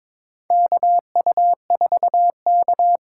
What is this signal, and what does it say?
Morse code, heard as a keyed tone, about 22 wpm, 700 Hz. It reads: KU4K
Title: USA Call Signs - 4 Characters - 3x Character Spacing - 22wpm